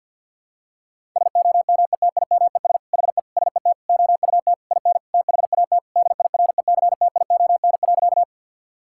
Morse code, heard as a keyed tone, about 51 wpm, 700 Hz. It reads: SOMETIMES HE SET OUT IN THAT DIRECTION.
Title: Sentences from Top 1000 Words - Review Part 5 - 50wpm